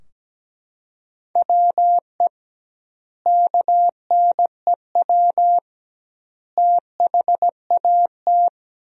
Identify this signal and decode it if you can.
Morse code, heard as a keyed tone, about 17 wpm, 700 Hz. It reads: WE KNEW THAT
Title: 2x Word-Spacing - Sentences from Top 400 Words - 17wpm